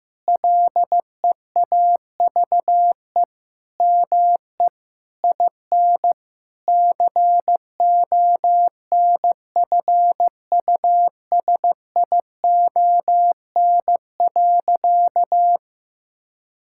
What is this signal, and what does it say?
Morse code, heard as a keyed tone, about 15 wpm, 700 Hz. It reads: LEAVE ME IN CONFUSION.
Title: Sentences from Top 1600 Words 15wpm